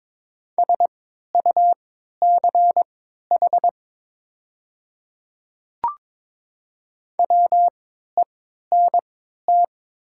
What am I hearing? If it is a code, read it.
Morse code, heard as a keyed tone, about 22 wpm, spaced as for 13 wpm, 700 Hz. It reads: SUCH WENT
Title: 3x Character Spacing - Top 200 Words - 22wpm